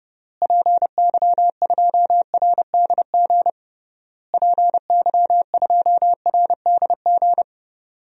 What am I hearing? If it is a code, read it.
Morse code, heard as a keyed tone, about 32 wpm, 700 Hz. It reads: PY2RDG PY2RDG
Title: DXCC-100 Call Signs - 6 Characters - Repeated 3x - 30wpm